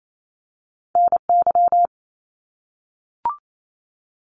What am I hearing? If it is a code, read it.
Morse code, heard as a keyed tone, about 28 wpm, 700 Hz. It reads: N Y
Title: US States 28wpm